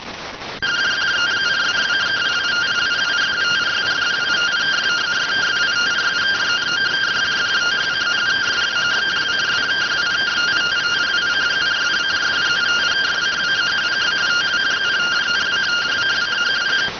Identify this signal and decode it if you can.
Two-tone teleprinter signal, about 170 Hz shift.